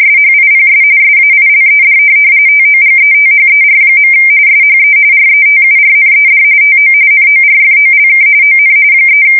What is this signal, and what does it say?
RTTY, about 45.45 baud. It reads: SHIPS SAILED TOGETHER. SHE RETURNED TO NORFOLK ON 3
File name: fsk170 / USS-Missouri